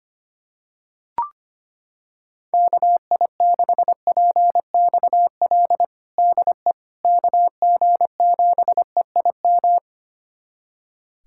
Morse code, heard as a keyed tone, about 25 wpm, 700 Hz. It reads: KI6PXL DE KG7EIM
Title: USA Callsign DE Callsign 25wpm